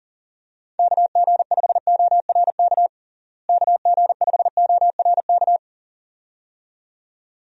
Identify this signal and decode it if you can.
Morse code, heard as a keyed tone, about 37 wpm, 700 Hz. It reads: KG5ORK KG5ORK
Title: USA Call Signs - 6 Characters - Repeated 2x - 40wpm